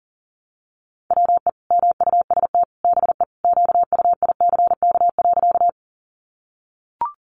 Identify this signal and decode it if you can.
Morse code, heard as a keyed tone, about 40 wpm, 700 Hz. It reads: WE MUST BE QUICK.
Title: Sentences from Top 500 Words 40wpm